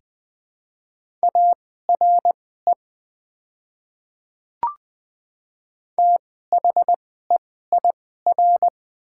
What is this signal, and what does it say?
Morse code, heard as a keyed tone, about 20 wpm, spaced as for 15 wpm, 700 Hz. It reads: ARE THEIR
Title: 2x Character Spacing - Top 100 Words - 20wpm